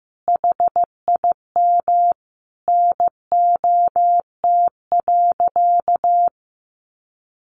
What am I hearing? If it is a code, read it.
Morse code, heard as a keyed tone, about 15 wpm, 700 Hz. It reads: HIM NOT.
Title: Sentences from Top 1200 Words 15wpm